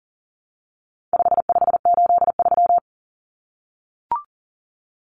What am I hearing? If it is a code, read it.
Morse code, heard as a keyed tone, about 40 wpm, 700 Hz. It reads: 5583
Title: Common Cut Numbers - 4-Digit 40wpm